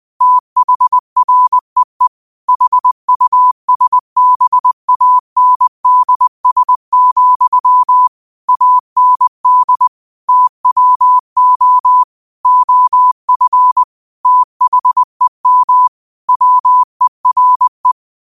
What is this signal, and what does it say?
Morse code, heard as a keyed tone, about 20 wpm, 1000 Hz. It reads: THREE HUSBANDS, AND TWO OF THEM WERE